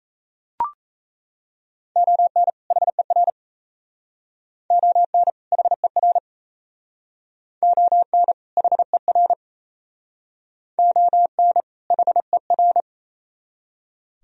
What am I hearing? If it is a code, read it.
Morse code, heard as a keyed tone, about 35 wpm, 700 Hz. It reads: ON HER ON HER ON HER ON HER
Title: Speed-Racing - Sets of 2 Words 28wpm